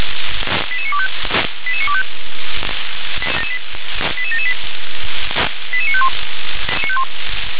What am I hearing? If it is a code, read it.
Five-tone selective call (ZVEI), heard as quick runs of five tones, tones of about 70 ms. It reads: 88925 88925 88778 88951